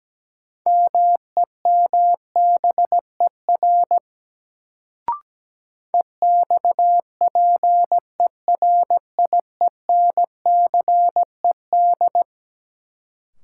Morse code, heard as a keyed tone, about 17 wpm, 700 Hz. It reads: MEMBER EXPERIENCED
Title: Sentences from Top 1100 Words - New Word Drill - 17wpm